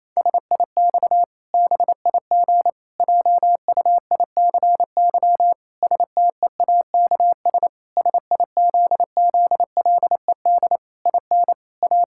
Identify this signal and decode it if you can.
Morse code, heard as a keyed tone, about 28 wpm, 700 Hz. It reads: SIX BIG JUICY STEAKS SIZZLED IN A
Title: Pangram Challenge v2 - 28wpm